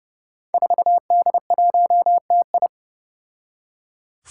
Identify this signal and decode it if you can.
Morse code, heard as a keyed tone, about 30 wpm, 700 Hz. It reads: 4D1TI
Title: Five Challenging Letter-Number Combinations 30wpm